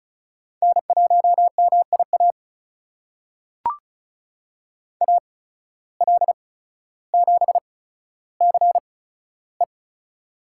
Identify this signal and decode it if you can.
Morse code, heard as a keyed tone, about 36 wpm, spaced as for 15 wpm, 700 Hz. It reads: N1MIA AL7CE
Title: USA Call Signs - 5 Characters - 8x Character Spacing - 35wpm